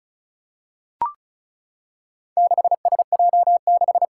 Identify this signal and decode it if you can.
Morse code, heard as a keyed tone, about 36 wpm, 700 Hz. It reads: 6SJ6
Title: Four Challenging Letter-Number Combinations 35wpm